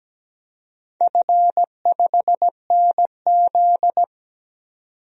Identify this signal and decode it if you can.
Morse code, heard as a keyed tone, about 17 wpm, 700 Hz. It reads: F5NZ
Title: DXCC-100 Call Signs - 4 Characters - 17wpm